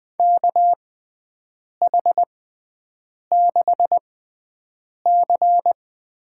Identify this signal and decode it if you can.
Morse code, heard as a keyed tone, about 20 wpm, spaced as for 7 wpm, 700 Hz. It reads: KH6C
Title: USA Call Signs - 6x Character Spacing - 20wpm